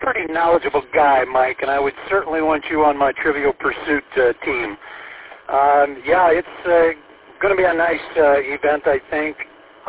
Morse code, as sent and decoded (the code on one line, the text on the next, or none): none